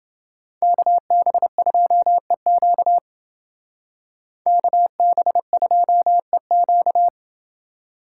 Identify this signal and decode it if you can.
Morse code, heard as a keyed tone, about 28 wpm, 700 Hz. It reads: KB2EQ KB2EQ